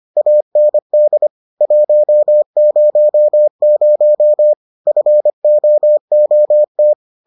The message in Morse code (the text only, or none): AND 100 FOOT